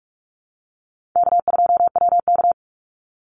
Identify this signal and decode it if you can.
Morse code, heard as a keyed tone, about 44 wpm, 700 Hz. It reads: K2WK
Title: Call Signs - 45wpm